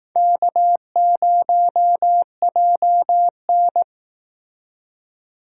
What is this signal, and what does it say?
Morse code, heard as a keyed tone, about 18 wpm, 700 Hz. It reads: K0JN
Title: Speed Racing - USA 1x2 Call Signs 15wpm